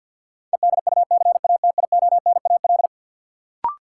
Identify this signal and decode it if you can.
Morse code, heard as a keyed tone, about 51 wpm, 700 Hz. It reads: EDUCATIONAL
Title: Sentences from Top 1200 Words - New Word Drill - 50wpm